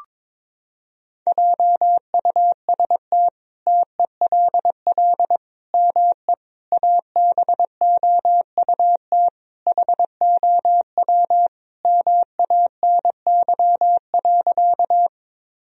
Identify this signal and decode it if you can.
Morse code, heard as a keyed tone, about 22 wpm, 700 Hz. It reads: JUST TELL ME ABOUT HOW MANY.
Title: Sentences from Top 1000 Words - Review Part 1 - 22wpm